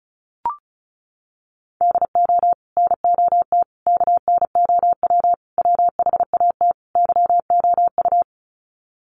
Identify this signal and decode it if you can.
Morse code, heard as a keyed tone, about 34 wpm, 700 Hz. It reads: DO NOT KNOW WHAT YOU
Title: Sets of 5 Words 35wpm